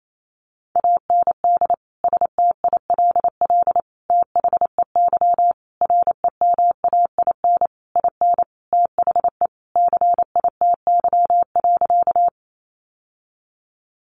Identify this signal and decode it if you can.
Morse code, heard as a keyed tone, about 28 wpm, 700 Hz. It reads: AND STILL THEY REMAIN IN THE CITY.
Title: Sentences from Top 1200 Words 28wpm